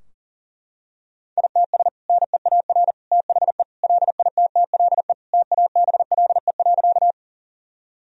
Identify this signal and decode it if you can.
Morse code, heard as a keyed tone, about 40 wpm, 700 Hz. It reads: ITS NEAR THE LITTLE TABLE.
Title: Sentences from Top 1000 Words - Review Part 2 - 40wpm